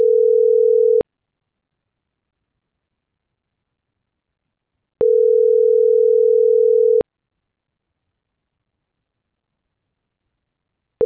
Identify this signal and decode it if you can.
Ringback tone (North America), 440 + 480 Hz.